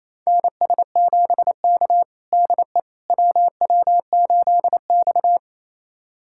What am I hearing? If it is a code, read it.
Morse code, heard as a keyed tone, about 28 wpm, 700 Hz. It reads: NS7K DE WW8X
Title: USA Callsign DE Callsign - 4 Characters - 28wpm